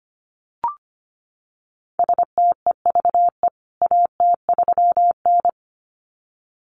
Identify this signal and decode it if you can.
Morse code, heard as a keyed tone, about 25 wpm, 700 Hz. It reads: STEVE AT3N
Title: CWT Exchange 25wpm